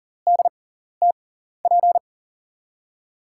Morse code, heard as a keyed tone, about 40 wpm, spaced as for 14 wpm, 700 Hz. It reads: DTP
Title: USA Call Signs - 5 Characters - 6x Character Spacing - 40wpm